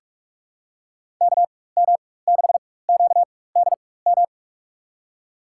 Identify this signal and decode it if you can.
Morse code, heard as a keyed tone, about 42 wpm, spaced as for 21 wpm, 700 Hz. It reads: KM6QDM
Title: USA Call Signs - 4x Character Spacing - 45wpm